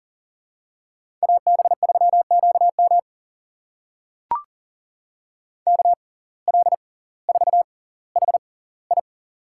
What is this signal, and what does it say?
Morse code, heard as a keyed tone, about 40 wpm, spaced as for 20 wpm, 700 Hz. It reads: AB3QM KL4HI
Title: USA Call Signs - 6x Character Spacing - 40wpm